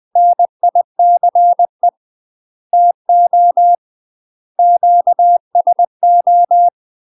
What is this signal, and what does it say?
Morse code, heard as a keyed tone, about 20 wpm, 700 Hz. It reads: NICE TO QSO
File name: UK MASTER QSOs TRAINING / 20WPM / UK MASTER QSOs 0002